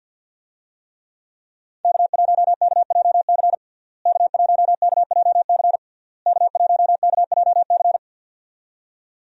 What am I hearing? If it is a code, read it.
Morse code, heard as a keyed tone, about 52 wpm, 700 Hz. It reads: K1KJC K1KJC K1KJC
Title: USA Call Signs - 5 Characters - Repeated 3x - 50wpm